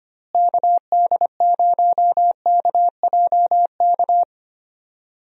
Morse code, heard as a keyed tone, about 25 wpm, 700 Hz. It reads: KD0KJK
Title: USA Call Signs - 6 Characters - Repeated 3x - 25wpm